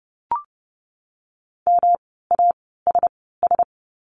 Morse code, heard as a keyed tone, about 30 wpm, spaced as for 17 wpm, 700 Hz. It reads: MASS
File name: course / 30:10wpm / 108 Lesson 18f - Letter m - Words 30wpm